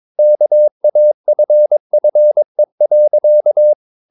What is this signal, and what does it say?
Morse code, heard as a keyed tone, about 22 wpm, 600 Hz. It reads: KAFFE.
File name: Ferie QRS 22wpm